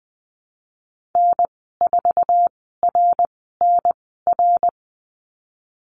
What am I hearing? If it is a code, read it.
Morse code, heard as a keyed tone, about 20 wpm, spaced as for 15 wpm, 700 Hz. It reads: N4RNR
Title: 062 Lesson 11g - Letter h - Call Signs 20wpm